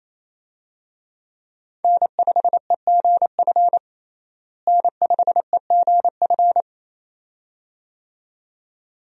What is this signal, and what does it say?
Morse code, heard as a keyed tone, about 27 wpm, 700 Hz. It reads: N5EGF N5EGF
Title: USA Call Signs - 5 Characters - Repeated 2x - 28wpm